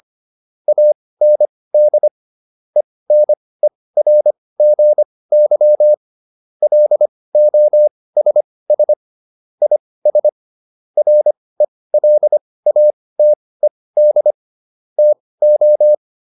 Morse code, heard as a keyed tone, about 25 wpm, spaced as for 18 wpm, 600 Hz. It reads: AND ENERGY LOSS IS RELATED TO
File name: SS211 18